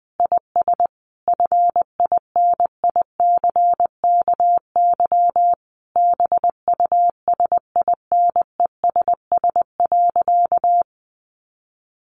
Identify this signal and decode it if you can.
Morse code, heard as a keyed tone, about 20 wpm, 700 Hz. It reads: IS FINICKY BUSINESS.
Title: A Sound Of Thunder 20wpm